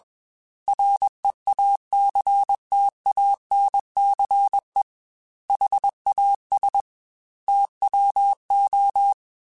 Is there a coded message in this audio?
Yes: Morse code from 0.7 to 9.1 s, heard as a keyed tone, about 21 wpm, 800 Hz. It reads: REACTANCE HAS TWO